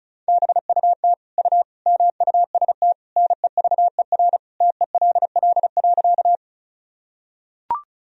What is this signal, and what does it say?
Morse code, heard as a keyed tone, about 36 wpm, 700 Hz. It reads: BUT U MUST NEVER TELL.
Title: Sentences from Top 300 Words and Abbreviations 35wpm